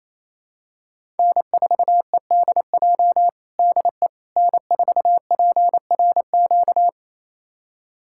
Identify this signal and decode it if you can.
Morse code, heard as a keyed tone, about 28 wpm, 700 Hz. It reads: N4EDJ DE N4PRQ
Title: USA Callsign DE Callsign - 5 Characters - 28wpm